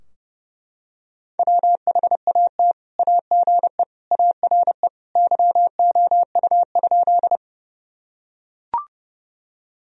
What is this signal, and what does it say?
Morse code, heard as a keyed tone, about 32 wpm, 700 Hz. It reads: WHAT AGE ARE YOU?
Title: Sentences from Top 1000 Words - Review Part 3 - 30wpm